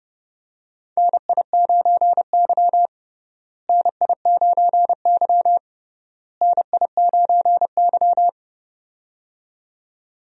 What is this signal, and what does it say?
Morse code, heard as a keyed tone, about 30 wpm, 700 Hz. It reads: NI9Y NI9Y NI9Y
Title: USA Call Signs - 4 Characters - Repeated 3x - 30wpm